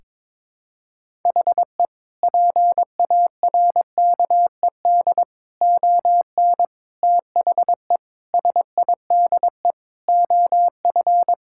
Morse code, heard as a keyed tone, about 22 wpm, 700 Hz. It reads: HE PARKED ON THE SIDE OF
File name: practice / 22wpm / Sentences with 4-Word Phrases - 22wpm